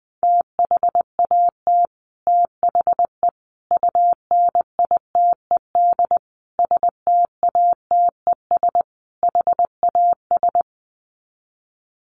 Morse code, heard as a keyed tone, about 20 wpm, 700 Hz. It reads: THAT THE UNITED STATES HAS
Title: Sets of 5 Words Encore 20wpm